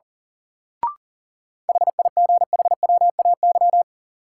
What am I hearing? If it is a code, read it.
Morse code, heard as a keyed tone, about 37 wpm, 700 Hz. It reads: HIGHWAY
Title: Sentences from Top 1400 Words - New Word Drill - 40wpm